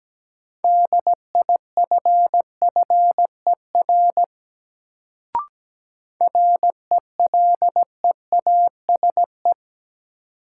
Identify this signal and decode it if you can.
Morse code, heard as a keyed tone, about 17 wpm, 700 Hz. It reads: DIFFER RELEASE
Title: Sentences from Top 1100 Words - New Word Drill Alt - 17wpm